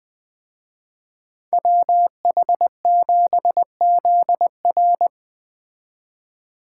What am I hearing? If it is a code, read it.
Morse code, heard as a keyed tone, about 20 wpm, 700 Hz. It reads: WH7ZR
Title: Speed Racing - USA 2x2 Call Signs 17wpm